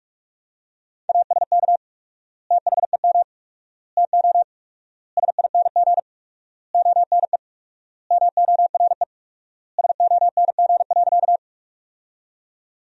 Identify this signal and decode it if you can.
Morse code, heard as a keyed tone, about 46 wpm, 700 Hz. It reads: ASK THEM TO SING ONE MORE SONG.
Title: 4x Word-Spacing - Sentences from Top 400 Words - 45wpm